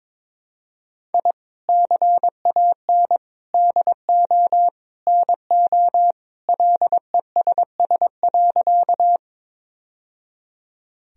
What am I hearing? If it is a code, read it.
Morse code, heard as a keyed tone, about 22 wpm, 700 Hz. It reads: I CAN DO NO LESS.
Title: Sentences from Top 1000 Words - Review Part 2 - 22wpm